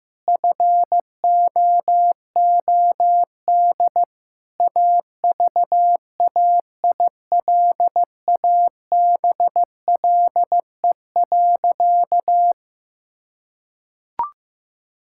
Morse code, heard as a keyed tone, about 16 wpm, 700 Hz. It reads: FOOD AVAILABLE.
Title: Sentences from Top 1000 Words - Review Part 6 - 15wpm